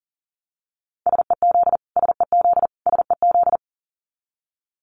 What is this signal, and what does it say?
Morse code, heard as a keyed tone, about 40 wpm, 700 Hz. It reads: SEZ SEZ SEZ
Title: Abbreviations Repeated 3x - 40wpm